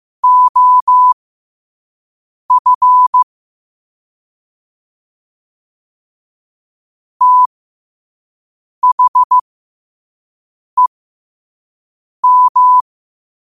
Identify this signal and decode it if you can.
Morse code, heard as a keyed tone, about 15 wpm, spaced as for 5 wpm, 1000 Hz. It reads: OF THEM
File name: qotdfiles / 5wpm